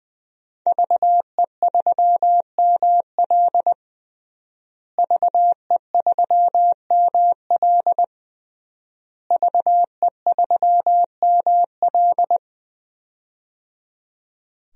Morse code, heard as a keyed tone, about 20 wpm, 700 Hz. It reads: VE3ML VE3ML VE3ML
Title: DXCC-100 Call Signs - 5 Characters - Repeated 3x - 20wpm